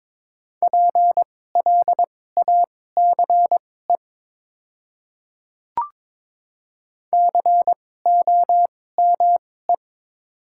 Morse code, heard as a keyed tone, about 22 wpm, spaced as for 16 wpm, 700 Hz. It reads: PLACE COME
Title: 2x Character Spacing - Top 200 Words - 22wpm